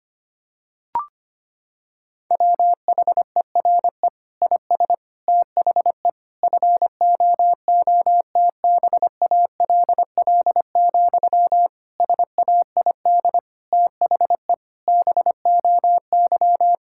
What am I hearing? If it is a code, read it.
Morse code, heard as a keyed tone, about 25 wpm, 700 Hz. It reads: WHERE IS THE FOOTBALL, SAID THE BOY
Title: Sentences from Top 1600 Words 25wpm